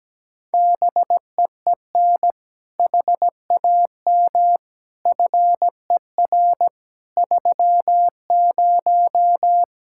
Morse code, heard as a keyed tone, about 17 wpm, 700 Hz. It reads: BEEN HAM FER 30
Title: Speed-Racing Common QSO Phrases Encore - 17wpm